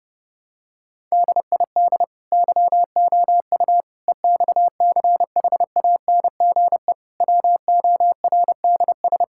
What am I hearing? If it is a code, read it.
Morse code, heard as a keyed tone, about 30 wpm, 700 Hz. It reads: DID YOU EXCHANGE WORDS
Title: Sentences from Top 1000 Words - Review Part 6 - 30wpm